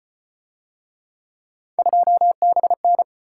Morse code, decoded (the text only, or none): 2BN